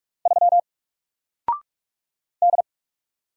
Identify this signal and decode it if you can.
Morse code, heard as a keyed tone, about 45 wpm, 700 Hz. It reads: 3 D
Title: Single Character (Letter:Number:Punctuation:Prosign) Alt 45wpm